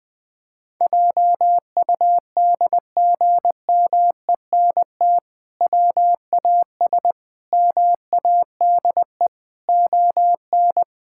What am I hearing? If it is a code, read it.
Morse code, heard as a keyed tone, about 20 wpm, 700 Hz. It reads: JUDGMENT WAS MADE ON
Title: Sentences with 5-Word Phrases - Spoken Beginning - 20wpm